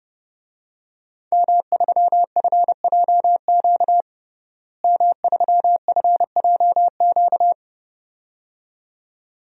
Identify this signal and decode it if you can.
Morse code, heard as a keyed tone, about 30 wpm, 700 Hz. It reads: M3FJQ M3FJQ
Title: DXCC-100 Call Signs - 5 Characters - Repeated 2x - 30wpm